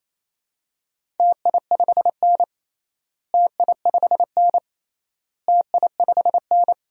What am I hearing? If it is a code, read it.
Morse code, heard as a keyed tone, about 28 wpm, 700 Hz. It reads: TI5N TI5N TI5N
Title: DXCC-100 Call Signs - 4 Characters - Repeated 3x - 28wpm